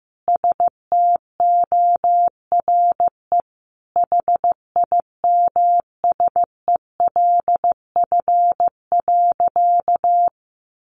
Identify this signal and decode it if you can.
Morse code, heard as a keyed tone, about 15 wpm, 700 Hz. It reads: STORE HIMSELF.
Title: Easy Phrases 15wpm